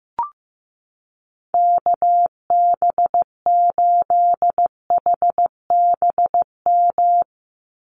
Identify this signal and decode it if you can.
Morse code, heard as a keyed tone, about 15 wpm, 700 Hz. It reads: KB8HBM
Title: USA Call Signs 15wpm